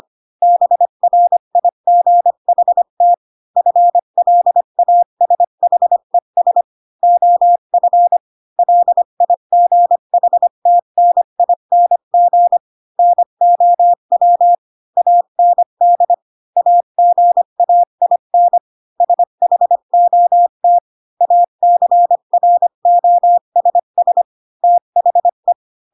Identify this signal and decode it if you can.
Morse code, heard as a keyed tone, about 25 wpm, 700 Hz. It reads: BRIGHT FLASHES OF LIGHTNING NOW AND AGAIN SHOT ACROSS THE
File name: Pinocchio / Pinocchio ch6 20 25 ik1hge